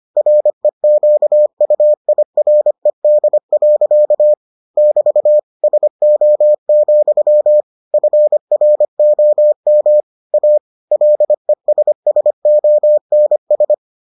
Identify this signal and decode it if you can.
Morse code, heard as a keyed tone, about 25 wpm, 600 Hz. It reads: REQUIRED. <BT> SO, FROM A LESSONS